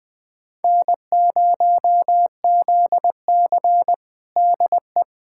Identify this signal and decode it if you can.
Morse code, heard as a keyed tone, about 20 wpm, 700 Hz. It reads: N0ZC DE
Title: USA Callsign DE Callsign - 4 Characters - 20wpm